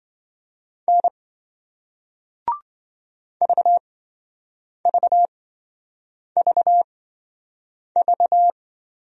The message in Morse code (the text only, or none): N VVVV